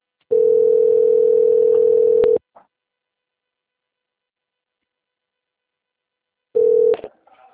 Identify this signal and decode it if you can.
Ringback tone (North America), 440 + 480 Hz.